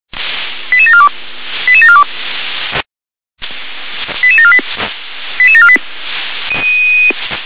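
Five-tone selective call (ZVEI), heard as quick runs of five tones, tones of about 70 ms. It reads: 88742 88742 88748 88748
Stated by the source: ZVEI five-tone selective call 88748, received by SDR.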